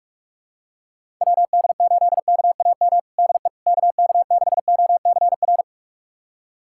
Morse code, heard as a keyed tone, about 46 wpm, 700 Hz. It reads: WD8KAM DE KK6OCR
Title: USA Callsign DE Callsign 45wpm